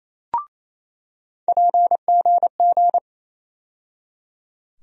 Morse code, heard as a keyed tone, about 28 wpm, 700 Hz. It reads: PGG
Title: USA Call Sign Postfixes - 3 Letters 28wpm